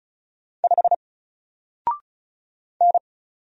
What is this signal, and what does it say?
Morse code, heard as a keyed tone, about 36 wpm, 700 Hz. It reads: 5 N
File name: Single-Letter-Number-35wpm